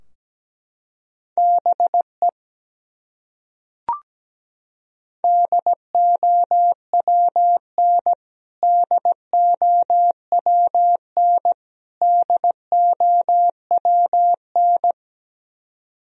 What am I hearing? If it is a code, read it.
Morse code, heard as a keyed tone, about 17 wpm, 700 Hz. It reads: BE DOWN DOWN DOWN
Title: Top 100 Words Repeated 17wpm